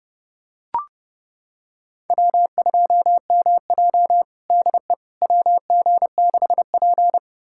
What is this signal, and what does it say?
Morse code, heard as a keyed tone, about 30 wpm, 700 Hz. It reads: W2MJ DE WG6P
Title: USA Callsign DE Callsign - 4 Characters - 30wpm